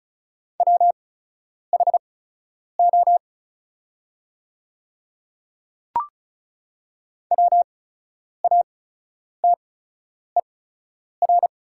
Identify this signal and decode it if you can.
Morse code, heard as a keyed tone, about 34 wpm, spaced as for 10 wpm, 690 Hz. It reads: WHO WATER
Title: 8x Character Spacing - Top 100 Words - 35wpm